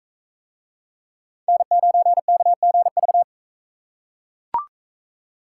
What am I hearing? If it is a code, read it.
Morse code, heard as a keyed tone, about 41 wpm, 700 Hz. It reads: N9KGV